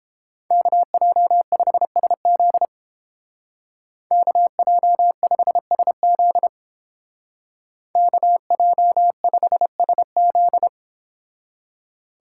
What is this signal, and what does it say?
Morse code, heard as a keyed tone, about 29 wpm, 700 Hz. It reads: KJ5SZ KJ5SZ KJ5SZ